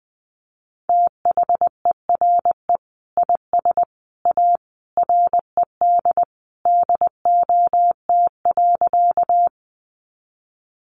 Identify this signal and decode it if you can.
Morse code, heard as a keyed tone, about 20 wpm, 700 Hz. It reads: THERE IS A RED DOT.
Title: Sentences from Top 1500 Words 20wpm